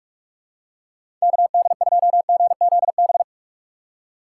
Morse code, heard as a keyed tone, about 44 wpm, 700 Hz. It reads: KD2GZB
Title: USA Call Signs - 6 Characters - 6x Character Spacing - 45wpm